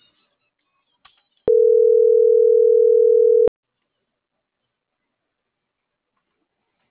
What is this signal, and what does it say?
Ringback tone (North America), 440 + 480 Hz.